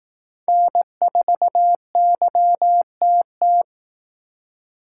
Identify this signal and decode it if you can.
Morse code, heard as a keyed tone, about 18 wpm, 700 Hz. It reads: N4YTT
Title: Speed Racing - USA 1x3 Call Signs 15wpm